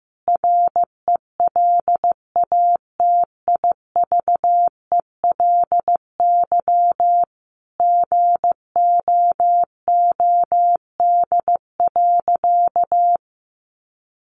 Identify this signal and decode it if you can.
Morse code, heard as a keyed tone, about 15 wpm, 700 Hz. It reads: RELATIVELY GOOD.
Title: Sentences from Top 1100 Words 15wpm